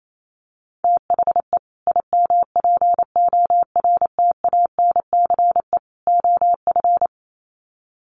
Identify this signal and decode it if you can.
Morse code, heard as a keyed tone, about 28 wpm, 700 Hz. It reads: THE IMPORTANCE OF
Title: Sets of 3 Words 28wpm